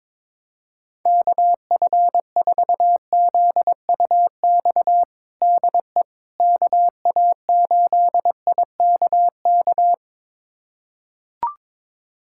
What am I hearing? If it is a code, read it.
Morse code, heard as a keyed tone, about 22 wpm, 700 Hz. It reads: KF4ZUX DE KA8IKK
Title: USA Callsign DE Callsign - 6 Characters - 22wpm